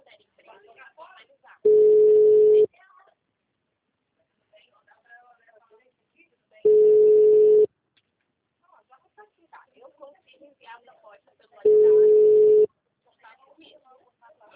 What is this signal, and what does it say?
Ringback tone (Europe, 425 Hz), with speech.